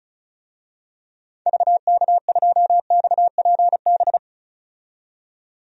Morse code, heard as a keyed tone, about 34 wpm, 700 Hz. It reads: VK2XPB
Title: DXCC-100 Call Signs - 6 Characters - 35wpm